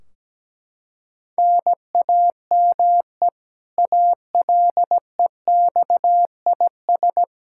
Morse code, heard as a keyed tone, about 17 wpm, 700 Hz. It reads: NAME ALEXIS
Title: Practice QSO - Exchange 1 - Beginning - 17wpm